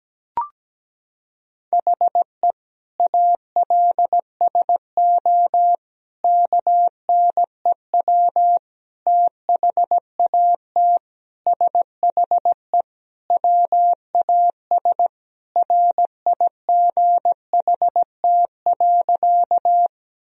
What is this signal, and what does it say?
Morse code, heard as a keyed tone, about 17 wpm, 700 Hz. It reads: HE ALSO KNEW THAT SHE WAS RIGHT.
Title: Sentences from Top 1000 Words - Review Part 1 - 17wpm